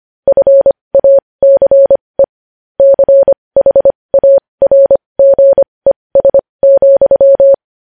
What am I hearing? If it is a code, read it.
Morse code, heard as a keyed tone, about 25 wpm, spaced as for 20 wpm, 550 Hz. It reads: FACE CHARGES,
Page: news-headlines-in-morse-code-at-20-wpm